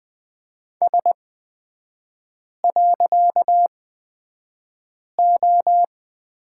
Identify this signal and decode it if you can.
Morse code, heard as a keyed tone, about 20 wpm, spaced as for 5 wpm, 700 Hz. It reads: S.O
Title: 255 Lesson 41e - Period - Three Characters 20wpm